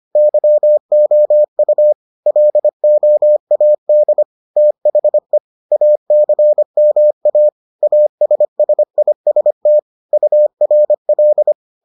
Morse code, heard as a keyed tone, about 25 wpm, 600 Hz. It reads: YOU LOAD THE ACMA ASSIST URL